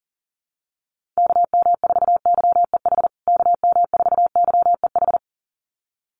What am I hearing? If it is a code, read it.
Morse code, heard as a keyed tone, about 37 wpm, 700 Hz. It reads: KM4YEH KM4YEH
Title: USA Call Signs Repeated 2x 40wpm